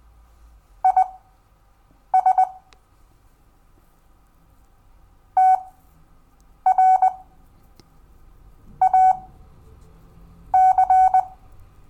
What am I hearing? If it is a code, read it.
Morse code, heard as a keyed tone, about 20 wpm, spaced as for 6 wpm, 750 Hz. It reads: IS TRAC
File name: cw / Lesson10 REVIEW